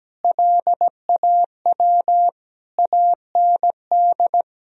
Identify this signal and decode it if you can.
Morse code, heard as a keyed tone, about 17 wpm, 700 Hz. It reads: LAW AND
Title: Sentences from Top 1000 Words - Review Part 3 - 17wpm